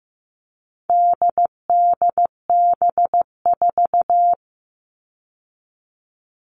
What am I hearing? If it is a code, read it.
Morse code, heard as a keyed tone, about 15 wpm, 700 Hz. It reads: DDB4
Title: All Cut Numbers - 4-Digit 15wpm